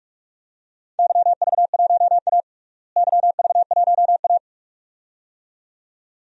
Morse code, heard as a keyed tone, about 47 wpm, 700 Hz. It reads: YV1A YV1A
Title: DXCC-100 Call Signs - 4 Characters - Repeated 2x - 45wpm